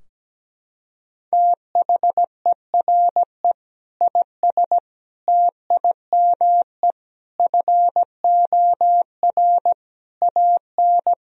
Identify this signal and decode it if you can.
Morse code, heard as a keyed tone, about 17 wpm, 700 Hz. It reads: THERE IS TIME FOR AN
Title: Sentences from Top 1000 Words - Review Part 3 - 17wpm